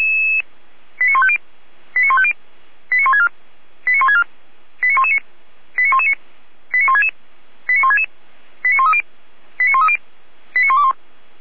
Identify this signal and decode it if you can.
Five-tone selective call (ZVEI), heard as quick runs of five tones, tones of about 70 ms. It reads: 89150 89150 89164 89164 89119 89119 89166 89166 89130 89130 89121
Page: SDR/ZVEI